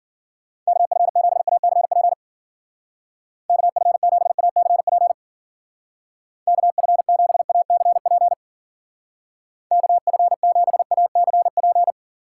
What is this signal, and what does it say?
Morse code, heard as a keyed tone, about 51 wpm, 700 Hz. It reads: KF7ACP KF7ACP KF7ACP KF7ACP